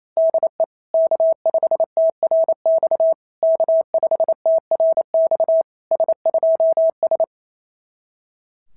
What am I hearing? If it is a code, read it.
Morse code, heard as a keyed tone, about 26 wpm, 650 Hz. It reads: DE K5TRX K5TRX S2S
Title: SOTA QSO Practice 28wpm